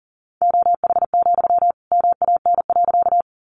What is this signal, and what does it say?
Morse code, heard as a keyed tone, about 40 wpm, 700 Hz. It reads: OH, MAN.
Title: Sentences from Top 500 Words 40wpm